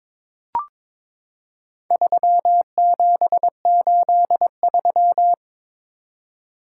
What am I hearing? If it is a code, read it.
Morse code, heard as a keyed tone, about 22 wpm, 700 Hz. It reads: 3783
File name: practice / 22wpm / Four Letter-Number Combinations - 22wpm